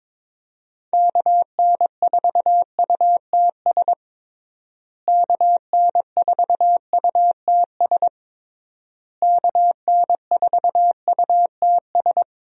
Morse code, heard as a keyed tone, about 21 wpm, 700 Hz. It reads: KN4UTS KN4UTS KN4UTS
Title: USA Call Signs - 6 Characters - Repeated 3x - 22wpm